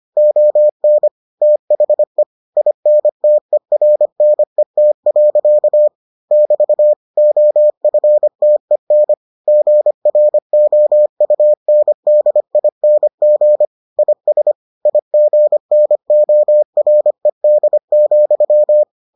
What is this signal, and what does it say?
Morse code, heard as a keyed tone, about 25 wpm, 600 Hz. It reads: ON THE INTERNET. <BT> OFTEN GROUNDING IS IGNORED,